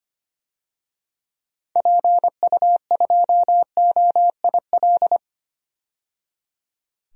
Morse code, heard as a keyed tone, about 25 wpm, 690 Hz. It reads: PU2OIL
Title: DXCC-100 Call Signs - 6 Characters - 25wpm